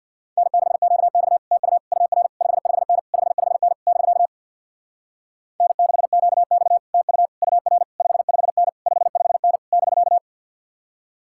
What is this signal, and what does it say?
Morse code, heard as a keyed tone, about 62 wpm, 690 Hz. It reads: N6QX TU UR 55N 55N <BK> N6QX TU UR 55N 55N <BK>
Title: Speed-Racing Common QSO Phrases - 50wpm